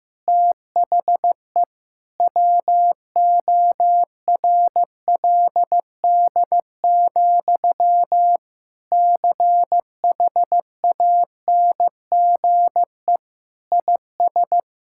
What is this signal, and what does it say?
Morse code, heard as a keyed tone, about 16 wpm, 700 Hz. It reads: THE WORLD, CHANGE IS
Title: Sentences from Top 1000 Words - Review Part 5 - 15wpm